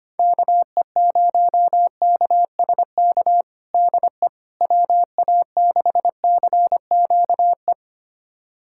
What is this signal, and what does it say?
Morse code, heard as a keyed tone, about 25 wpm, 700 Hz. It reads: KE0KSK DE WA6CQE
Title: USA Callsign DE Callsign 25wpm